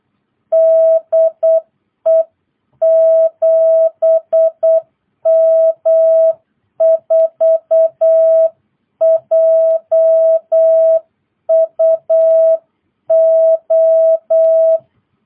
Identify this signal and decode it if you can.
Morse code, heard as a keyed tone, about 8 wpm, 660 Hz. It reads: DE 7M4JUO